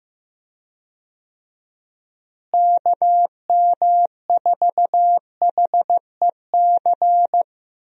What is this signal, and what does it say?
Morse code, heard as a keyed tone, about 15 wpm, 700 Hz. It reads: KM4HEC
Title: USA Call Signs - 6 Characters - Repeated 2x - 15wpm